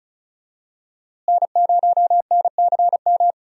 Morse code, heard as a keyed tone, about 36 wpm, 700 Hz. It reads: N0NCM
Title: USA Call Signs - 5 Characters - Repeated 3x - 35wpm